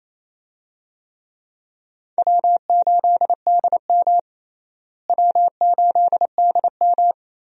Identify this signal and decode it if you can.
Morse code, heard as a keyed tone, about 28 wpm, 700 Hz. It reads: W8DM W8DM
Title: USA Call Signs - 4 Characters - Repeated 3x - 28wpm